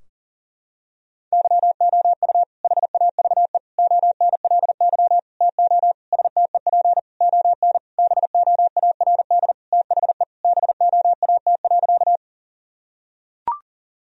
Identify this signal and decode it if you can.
Morse code, heard as a keyed tone, about 40 wpm, 700 Hz. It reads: YOU HAVE ONLY TO STEP ON BOARD THE BOAT.
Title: Sentences from Top 1000 Words - Review Part 4 - 40wpm